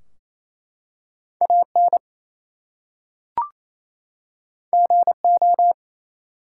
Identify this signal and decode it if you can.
Morse code, heard as a keyed tone, about 28 wpm, 700 Hz. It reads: AN GO
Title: Two Letter Words - 28wpm